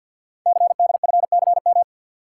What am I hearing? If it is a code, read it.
Morse code, heard as a keyed tone, about 48 wpm, 700 Hz. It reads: CDFCM
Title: Code Groups - 5 Letters - 50wpm